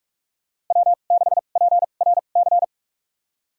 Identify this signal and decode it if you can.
Morse code, heard as a keyed tone, about 47 wpm, spaced as for 33 wpm, 700 Hz. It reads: W6PRC
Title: USA Call Signs - 2x Character Spacing - 45wpm